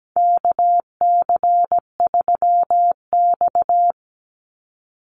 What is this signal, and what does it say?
Morse code, heard as a keyed tone, about 17 wpm, 700 Hz. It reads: KC3X
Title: Call Signs Repeated 2x - 17wpm